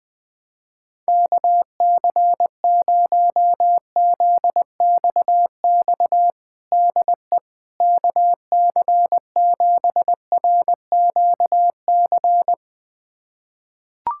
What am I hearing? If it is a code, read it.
Morse code, heard as a keyed tone, about 20 wpm, 700 Hz. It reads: KC0ZXX DE KC7RQC
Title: USA Callsign DE Callsign 20wpm